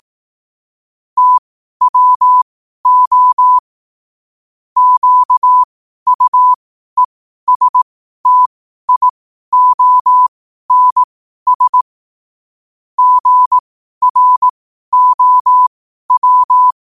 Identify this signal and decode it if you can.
Morse code, heard as a keyed tone, about 18 wpm, spaced as for 12 wpm, 1000 Hz. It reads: TWO QUESTIONS GROW
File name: qotdfiles / 12wpm